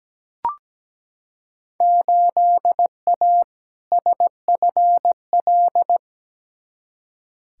Morse code, heard as a keyed tone, about 17 wpm, 700 Hz. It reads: 8A SFL
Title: Field Day Exchanges Encore - 17wpm